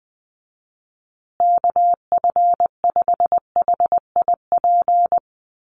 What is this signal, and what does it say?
Morse code, heard as a keyed tone, about 20 wpm, 700 Hz. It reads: KF5HIP
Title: USA Call Signs Repeated 3x 20wpm